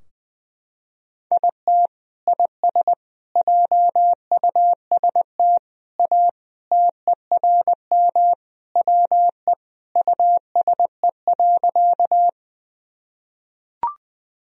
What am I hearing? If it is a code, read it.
Morse code, heard as a keyed tone, about 21 wpm, 700 Hz. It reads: IT IS JUST A TERM WE USE.
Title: Sentences from Top 1000 Words - Review Part 4 - 20wpm